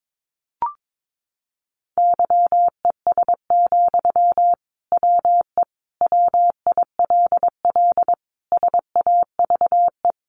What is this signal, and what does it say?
Morse code, heard as a keyed tone, about 22 wpm, 700 Hz. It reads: YES, WE WILL SAVE
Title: Sentences from Top 1100 Words 22wpm